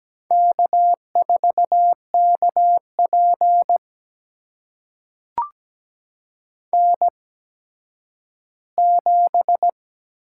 Morse code, heard as a keyed tone, about 17 wpm, spaced as for 10 wpm, 700 Hz. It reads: K4KP N7
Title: USA Call Signs - 4 Characters - 8x Character Spacing - 17wpm